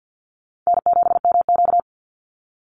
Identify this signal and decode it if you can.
Morse code, heard as a keyed tone, about 50 wpm, 700 Hz. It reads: N7MQ
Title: Call Signs Repeated 3x - 50wpm